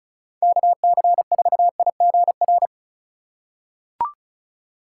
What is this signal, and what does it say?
Morse code, heard as a keyed tone, about 35 wpm, 700 Hz. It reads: KC4IGR
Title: USA Call Signs - 4x Character Spacing - 35wpm